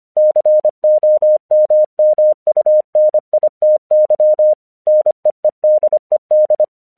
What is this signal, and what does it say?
Morse code, heard as a keyed tone, about 25 wpm, 610 Hz. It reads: COMMUNITY NEEDED
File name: 1 English 25wpm / AQC05